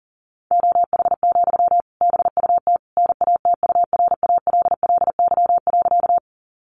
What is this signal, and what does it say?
Morse code, heard as a keyed tone, about 40 wpm, 700 Hz. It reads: OH, BUT NATURALLY.
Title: Sentences from Top 1300 Words 40wpm